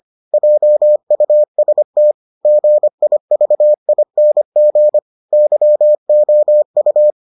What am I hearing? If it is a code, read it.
Morse code, heard as a keyed tone, about 25 wpm, 600 Hz. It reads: JUST GIVING YOU